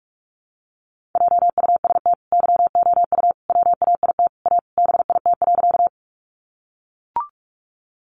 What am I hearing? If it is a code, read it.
Morse code, heard as a keyed tone, about 44 wpm, 700 Hz. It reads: JUST YOU WAIT A BIT.
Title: Sentences from Top 900 Words 45wpm